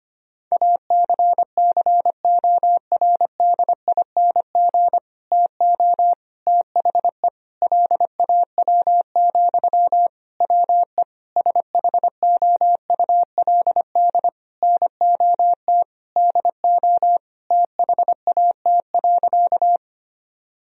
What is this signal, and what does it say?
Morse code, heard as a keyed tone, about 25 wpm, 700 Hz. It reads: ACCORDING TO THE LAW, WE SHOULD NOT DO THAT.
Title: Sentences from Top 1000 Words - Review Part 4 - 25wpm